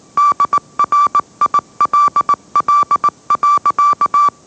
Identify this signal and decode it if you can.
Morse code, heard as a keyed tone, about 19 wpm, 1160 Hz. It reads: DRILL.
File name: mrx / mrx081 20wpm